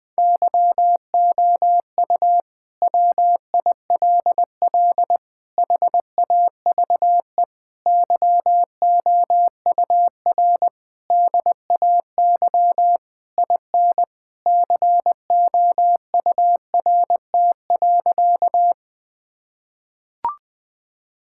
Morse code, heard as a keyed tone, about 20 wpm, 700 Hz. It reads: YOU WILL HAVE YOUR DAY IN COURT.
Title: Sentences from Top 1600 Words 20wpm